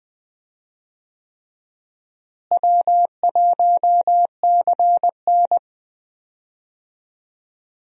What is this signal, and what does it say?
Morse code, heard as a keyed tone, about 20 wpm, 700 Hz. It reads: W1CN
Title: USA Call Signs - 4 Characters - Repeated 3x - 20wpm